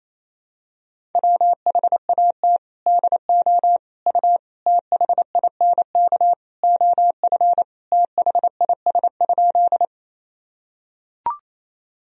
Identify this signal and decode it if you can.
Morse code, heard as a keyed tone, about 28 wpm, 700 Hz. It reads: WHAT DO U THINK OF THIS?
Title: Sentences from Top 200 Words and Abbreviations 28wpm